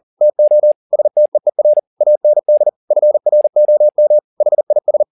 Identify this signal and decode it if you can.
Morse code, heard as a keyed tone, about 40 wpm, 600 Hz. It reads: TO STEER AND FROM HIS
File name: the old man and the sea 2 40wpm